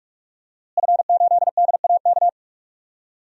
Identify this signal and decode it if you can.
Morse code, heard as a keyed tone, about 46 wpm, 700 Hz. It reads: F8DAK
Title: DXCC-100 Call Signs - 5 Characters - Repeated 3x - 45wpm